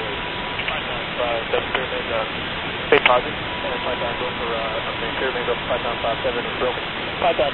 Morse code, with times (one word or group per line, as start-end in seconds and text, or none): none